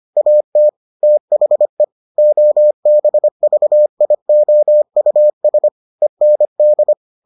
Morse code, heard as a keyed tone, about 25 wpm, 600 Hz. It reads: AT THE OBVIOUS END